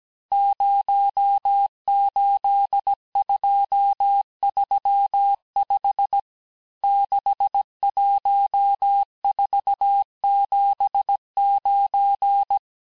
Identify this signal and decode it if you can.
Morse code, heard as a keyed tone, about 17 wpm, 800 Hz. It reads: 08235 61479
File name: corrige-cours-1581-texte-chiffre-vitesse-16-mpm